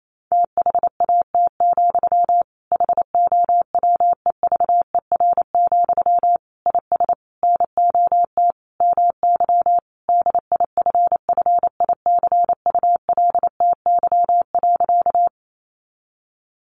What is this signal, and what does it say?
Morse code, heard as a keyed tone, about 28 wpm, 700 Hz. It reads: THAT, HOWEVER, IS NOT MY DIFFICULTY.
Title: Sentences from Top 1500 Words 28wpm